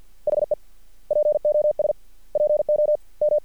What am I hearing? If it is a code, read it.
Morse code, heard as a keyed tone, about 50 wpm, 600 Hz. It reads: HE POS PO M
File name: archived / CRSnet